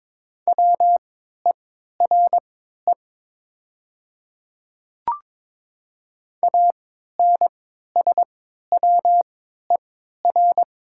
Morse code, heard as a keyed tone, about 22 wpm, spaced as for 12 wpm, 700 Hz. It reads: WERE ANSWER
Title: 3x Character Spacing - Top 300 Words - 22wpm